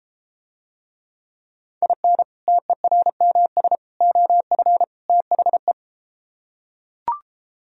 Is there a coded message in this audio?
Yes: Morse code from 1.8 to 5.7 s, heard as a keyed tone, about 33 wpm, 700 Hz. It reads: IN TERMS OF THE